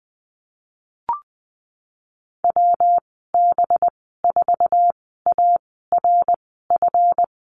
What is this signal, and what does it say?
Morse code, heard as a keyed tone, about 19 wpm, spaced as for 14 wpm, 700 Hz. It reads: WB4ARF
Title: 219 Lesson 35g - Number 0 - Call Signs 20wpm